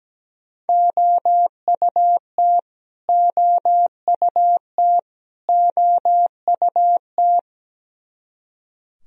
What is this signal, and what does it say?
Morse code, heard as a keyed tone, about 17 wpm, 700 Hz. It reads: OUT OUT OUT
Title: Top 100 Words Repeated 17wpm